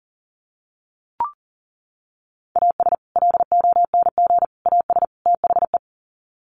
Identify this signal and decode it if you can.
Morse code, heard as a keyed tone, about 40 wpm, 700 Hz. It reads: AS LONG AS THE
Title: Sets of 4 Words 40wpm